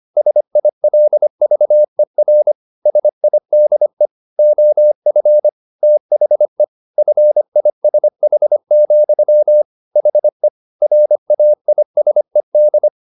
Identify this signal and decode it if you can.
Morse code, heard as a keyed tone, about 25 wpm, 600 Hz. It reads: SILVER SIDE OF THE FISH, HE RAISED